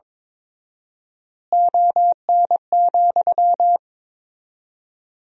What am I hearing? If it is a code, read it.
Morse code, heard as a keyed tone, about 22 wpm, 700 Hz. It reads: ON,
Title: 4x Word-Spacing - Sentences from Top 100 Words - 22wpm